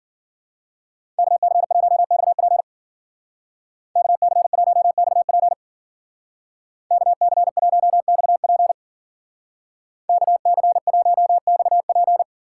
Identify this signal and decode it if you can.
Morse code, heard as a keyed tone, about 50 wpm, 700 Hz. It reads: KC1XP KC1XP KC1XP KC1XP
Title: Speed Racing - USA 2x2 Call Signs 40wpm